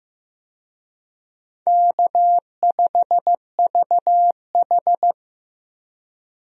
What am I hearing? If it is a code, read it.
Morse code, heard as a keyed tone, about 15 wpm, 700 Hz. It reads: K5VH
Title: USA Call Signs - 4 Characters - Repeated 3x - 15wpm